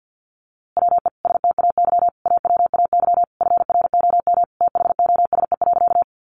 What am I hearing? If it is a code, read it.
Morse code, heard as a keyed tone, about 50 wpm, 700 Hz. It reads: WE STAY AWAY FROM THOSE.
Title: A Sound Of Thunder 50wpm